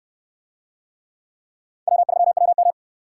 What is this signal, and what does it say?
Morse code, heard as a keyed tone, about 67 wpm, 700 Hz. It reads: W3WM